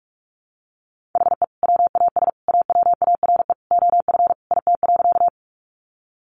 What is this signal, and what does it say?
Morse code, heard as a keyed tone, about 44 wpm, 700 Hz. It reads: HE WAS AWARE OF IT.
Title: Sentences from Top 1200 Words 45wpm